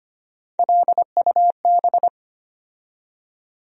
Morse code, heard as a keyed tone, about 25 wpm, 700 Hz. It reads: LUB
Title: USA Call Sign Postfixes - 3 Letters 25wpm